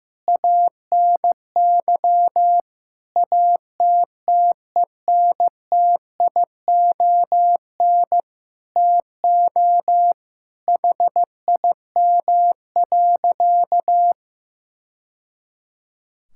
Morse code, heard as a keyed tone, about 15 wpm, 700 Hz. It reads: ANY ATTENTION TO HIM.
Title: Sentences from Top 1000 Words - Review Part 6 - 15wpm